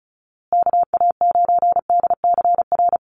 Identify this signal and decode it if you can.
Morse code, heard as a keyed tone, about 34 wpm, 700 Hz. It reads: KA9DCR
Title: USA Call Signs Repeated 2x 35wpm